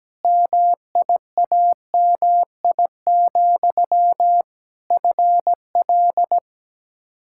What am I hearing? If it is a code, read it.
Morse code, heard as a keyed tone, about 17 wpm, 700 Hz. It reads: MIAMI, FL
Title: Practice QSO - Exchange 1 - Beginning - 17wpm